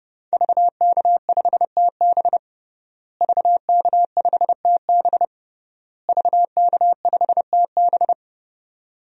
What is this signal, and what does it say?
Morse code, heard as a keyed tone, about 29 wpm, 700 Hz. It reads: VK5TB VK5TB VK5TB
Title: DXCC-100 Call Signs - 5 Characters - Repeated 3x - 30wpm